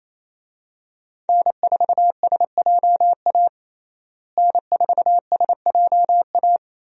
Morse code, heard as a keyed tone, about 28 wpm, 700 Hz. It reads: N4SJA N4SJA
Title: USA Call Signs - 5 Characters - Repeated 2x - 28wpm